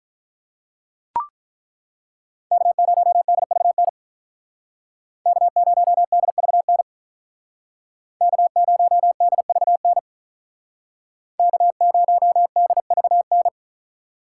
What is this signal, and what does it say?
Morse code, heard as a keyed tone, about 43 wpm, 700 Hz. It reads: K0DVN K0DVN K0DVN K0DVN
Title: Speed Racing - USA Call Signs 35wpm